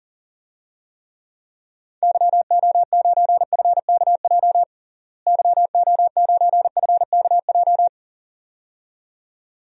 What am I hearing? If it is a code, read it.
Morse code, heard as a keyed tone, about 37 wpm, 700 Hz. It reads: YO9FKJ YO9FKJ
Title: DXCC-100 Call Signs - 6 Characters - Repeated 2x - 40wpm